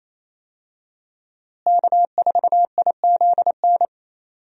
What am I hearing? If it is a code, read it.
Morse code, heard as a keyed tone, about 28 wpm, 700 Hz. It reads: K4IZN
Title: USA Call Signs - 5 Characters - Repeated 2x - 28wpm